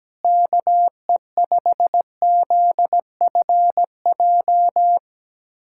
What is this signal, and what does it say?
Morse code, heard as a keyed tone, about 17 wpm, 700 Hz. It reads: KE5ZFJ
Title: USA Callsign DE Callsign 17wpm